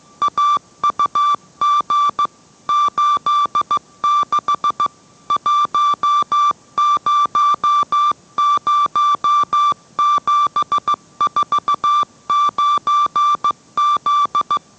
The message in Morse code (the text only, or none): AUG 86 100749Z